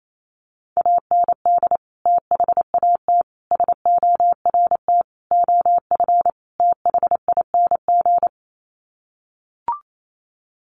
Morse code, heard as a keyed tone, about 28 wpm, 700 Hz. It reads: AND THAT SORT OF THING
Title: Sets of 5 Words Encore2 28wpm